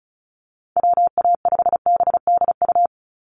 Morse code, heard as a keyed tone, about 34 wpm, 700 Hz. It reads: WA5BDU
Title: Call Signs - 35wpm